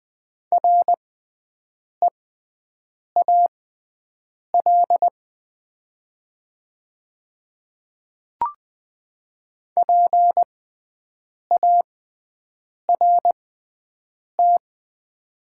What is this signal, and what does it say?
Morse code, heard as a keyed tone, about 20 wpm, spaced as for 7 wpm, 700 Hz. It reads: REAL PART
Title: 6x Character Spacing - Top 300 Words - 20wpm